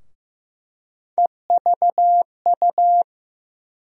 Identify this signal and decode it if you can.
Morse code, heard as a keyed tone, about 16 wpm, 700 Hz. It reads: EVU
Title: USA Call Sign Postfixes - 3 Letters 15wpm